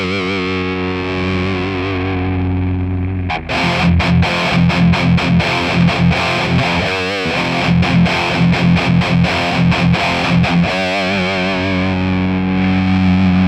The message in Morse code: none